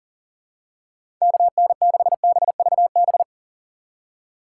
Morse code, heard as a keyed tone, about 40 wpm, 700 Hz. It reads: KN6BVB
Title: USA Call Signs - 6 Characters - 4x Character Spacing - 40wpm